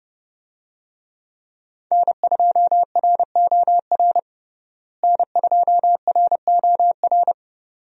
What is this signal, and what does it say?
Morse code, heard as a keyed tone, about 31 wpm, 700 Hz. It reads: N2ROR N2ROR
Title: USA Call Signs - 5 Characters - Repeated 2x - 30wpm